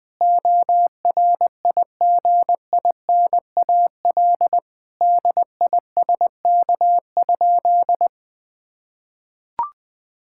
Morse code, heard as a keyed tone, about 21 wpm, 700 Hz. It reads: ORIGINAL DISK?
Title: Sentences from Top 1000 Words - Review Part 6 - 20wpm